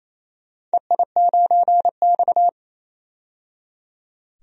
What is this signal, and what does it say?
Morse code, heard as a keyed tone, about 28 wpm, 700 Hz. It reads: EI9X
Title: DXCC-100 Call Signs - 4 Characters - 28wpm